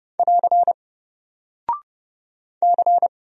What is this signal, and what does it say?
Morse code, heard as a keyed tone, about 30 wpm, 700 Hz. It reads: <AR> C
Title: Single Character (Letter:Number:Punctuation:Prosign) 30wpm